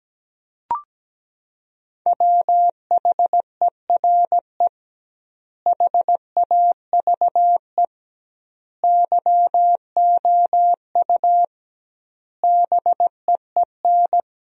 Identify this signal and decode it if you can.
Morse code, heard as a keyed tone, about 17 wpm, 700 Hz. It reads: WHERE HAVE YOU BEEN
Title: 2x Word-Spacing - Sentences from Top 300 Words - 17wpm